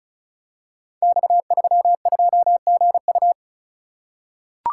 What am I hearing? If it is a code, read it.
Morse code, heard as a keyed tone, about 36 wpm, 700 Hz. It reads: X32GU
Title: Code Groups - 5 Letter:Numbers - 35wpm